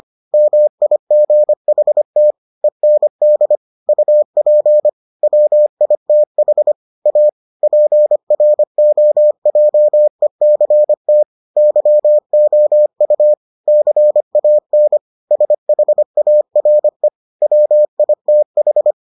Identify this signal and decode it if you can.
Morse code, heard as a keyed tone, about 25 wpm, 600 Hz. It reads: MIGHT END UP WITH A PROJECT YOU CAN SHARE WITH